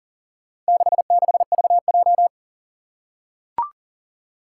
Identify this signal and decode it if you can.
Morse code, heard as a keyed tone, about 40 wpm, 700 Hz. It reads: 66VJ
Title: Four Challenging Letter-Number Combinations 40wpm